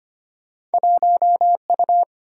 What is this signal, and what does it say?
Morse code, heard as a keyed tone, about 35 wpm, spaced as for 30 wpm, 700 Hz. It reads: 1U
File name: practice / 25wpm / Speed-Racing - Two Letter-Number Combinations - 25wpm